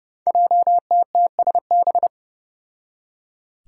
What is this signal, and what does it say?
Morse code, heard as a keyed tone, about 28 wpm, 700 Hz. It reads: JTTSB
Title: Five Challenging Letter-Number Combinations 30wpm